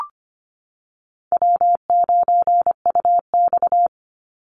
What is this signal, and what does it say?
Morse code, heard as a keyed tone, about 25 wpm, 700 Hz. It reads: W9UX
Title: NA SOTA Chaser Callsigns 25wpm